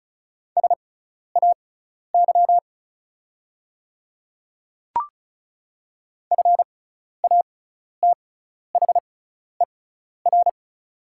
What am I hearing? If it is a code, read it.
Morse code, heard as a keyed tone, about 36 wpm, spaced as for 12 wpm, 700 Hz. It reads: SAY FATHER
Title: 6x Character Spacing - Top 200 Words - 35wpm